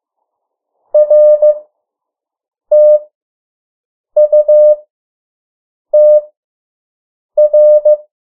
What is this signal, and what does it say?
Morse code, heard as a keyed tone, about 15 wpm, spaced as for 6 wpm, 600 Hz. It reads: RTUTR
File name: morse mp3 / 03 urok 2